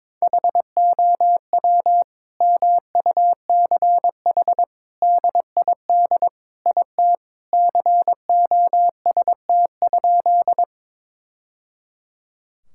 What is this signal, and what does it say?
Morse code, heard as a keyed tone, about 22 wpm, 700 Hz. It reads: HOW MUCH DID IT COST?
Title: Sentences from Top 1000 Words - Review Part 5 - 22wpm